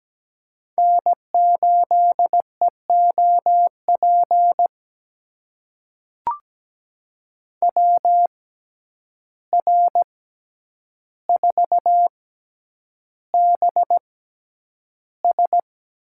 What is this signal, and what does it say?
Morse code, heard as a keyed tone, about 17 wpm, spaced as for 9 wpm, 700 Hz. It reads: N8EOP WR4BS
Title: USA Call Signs - 5 Characters - 6x Character Spacing - 17wpm